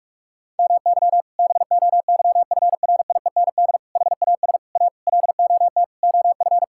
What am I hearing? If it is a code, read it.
Morse code, heard as a keyed tone, about 46 wpm, 700 Hz. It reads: MY BOYFRIEND HAS A LOT OF
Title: Sentences from Top 1000 Words - Review Part 5 - 45wpm